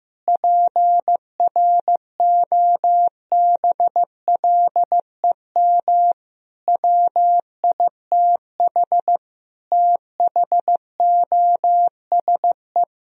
Morse code, heard as a keyed tone, about 15 wpm, 700 Hz. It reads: PROBLEM WITH THOSE
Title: Sentences from Top 1000 Words - Review Part 2 - 15wpm